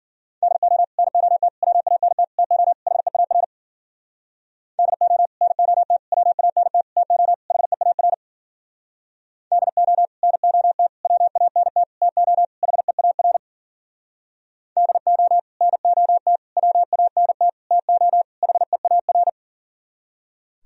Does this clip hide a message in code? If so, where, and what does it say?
Yes: Morse code from 0.4 to 19.3 s, heard as a keyed tone, about 51 wpm, 700 Hz. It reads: DO NOT WANT TO HEAR DO NOT WANT TO HEAR DO NOT WANT TO HEAR DO NOT WANT TO HEAR